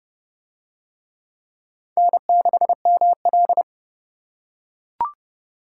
Morse code, heard as a keyed tone, about 29 wpm, 700 Hz. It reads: N6ML